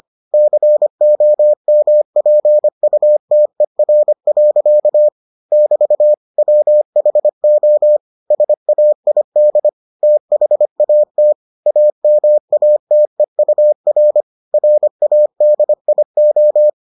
Morse code, heard as a keyed tone, about 25 wpm, 600 Hz. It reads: COMPUTER. <BT> WHO SAID THAT AMATEUR RADIO